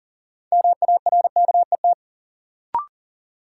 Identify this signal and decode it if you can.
Morse code, heard as a keyed tone, about 40 wpm, 700 Hz. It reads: MARKET
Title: Sentences from Top 600 Words - New Word Drill - 40wpm